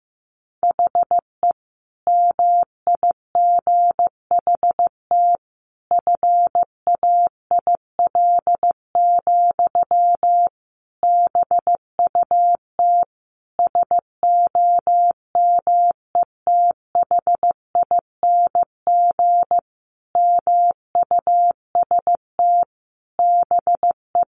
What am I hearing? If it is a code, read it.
Morse code, heard as a keyed tone, about 15 wpm, 700 Hz. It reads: HE MIGHT FAIL, BUT SOMETHING MUST BE
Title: Sentences from Top 1100 Words 15wpm